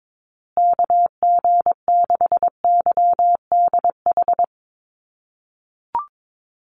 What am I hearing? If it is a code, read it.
Morse code, heard as a keyed tone, about 22 wpm, 700 Hz. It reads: KG6YDH
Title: USA Call Signs Repeated 3x 22wpm